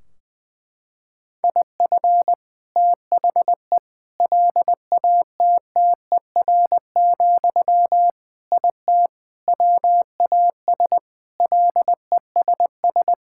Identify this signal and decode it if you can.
Morse code, heard as a keyed tone, about 20 wpm, 700 Hz. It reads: IF THE LATTER, IT WAS LESS
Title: Sentences from Top 1600 Words 20wpm